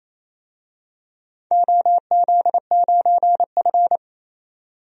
Morse code, heard as a keyed tone, about 28 wpm, 700 Hz. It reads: OZ9F
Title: DXCC-100 Call Signs - 4 Characters - 28wpm